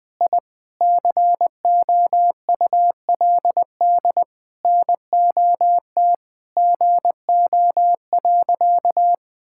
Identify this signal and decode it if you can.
Morse code, heard as a keyed tone, about 21 wpm, 700 Hz. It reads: I COULD NOT GO.
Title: Sentences with 5-Word Phrases - Spoken Beginning - 20wpm